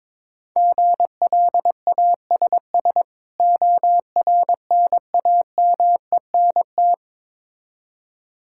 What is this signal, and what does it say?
Morse code, heard as a keyed tone, about 22 wpm, 690 Hz. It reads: GLASS ORNAMENT
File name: Difficult Words with Hints - 22wpm